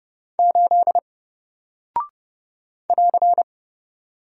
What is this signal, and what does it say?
Morse code, heard as a keyed tone, about 30 wpm, 700 Hz. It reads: 8 <AR>
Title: Single Character (Letter:Number:Punctuation:Prosign) Alt 30wpm